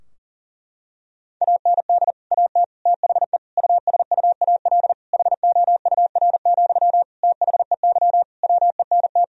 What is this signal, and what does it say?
Morse code, heard as a keyed tone, about 37 wpm, 700 Hz. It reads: AND AT THE USUAL HOUR, THEY WENT
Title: Sentences from Top 1000 Words - Review Part 2 - 40wpm